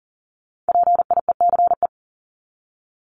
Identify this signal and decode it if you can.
Morse code, heard as a keyed tone, about 40 wpm, 700 Hz. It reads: PIECE
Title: Top 500 Words 40wpm